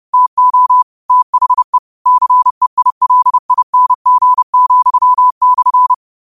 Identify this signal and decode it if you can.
Morse code, heard as a keyed tone, about 30 wpm, 1000 Hz. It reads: TO THE CEILING,/
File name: qotdfiles / 30wpm